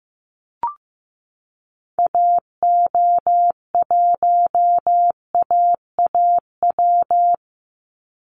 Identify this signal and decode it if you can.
Morse code, heard as a keyed tone, about 15 wpm, 700 Hz. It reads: AO1AAW
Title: Call Signs Repeated 3x - 15wpm